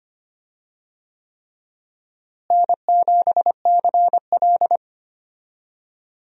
Morse code, heard as a keyed tone, about 25 wpm, 700 Hz. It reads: N7CL
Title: USA Call Signs - 4 Characters - Repeated 3x - 25wpm